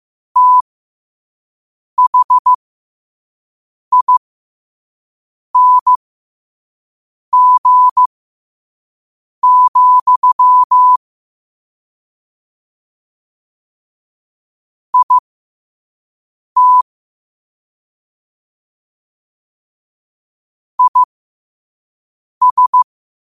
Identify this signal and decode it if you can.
Morse code, heard as a keyed tone, about 15 wpm, spaced as for 5 wpm, 1000 Hz. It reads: THING, IT IS
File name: qotdfiles / 5wpm